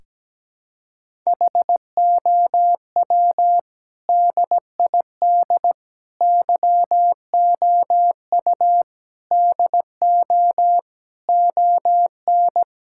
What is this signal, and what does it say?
Morse code, heard as a keyed tone, about 17 wpm, 700 Hz. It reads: HOW DID YOU DO ON
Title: Sentences from Top 1000 Words - Review Part 5 - 17wpm